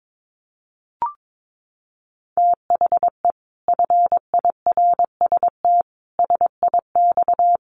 Morse code, heard as a keyed tone, about 22 wpm, 700 Hz. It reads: THE FIRST SIX
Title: Sets of 5 Words Encore2 22wpm